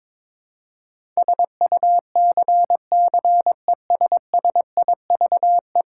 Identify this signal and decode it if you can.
Morse code, heard as a keyed tone, about 22 wpm, 700 Hz. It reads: SUCCESSIVE
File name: practice / 22wpm / 10 Letter Words - First 3 Letters Given - 22wpm